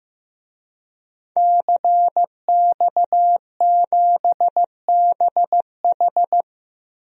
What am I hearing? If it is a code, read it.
Morse code, heard as a keyed tone, about 16 wpm, 700 Hz. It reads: CX7BH
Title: DXCC-100 Call Signs - 5 Characters - Repeated 2x - 15wpm